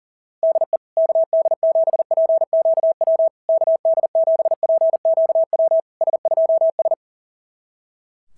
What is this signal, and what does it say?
Morse code, heard as a keyed tone, about 40 wpm, 690 Hz. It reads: DE KD7PQW KD7PQW S2S
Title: SOTA QSO Practice 40wpm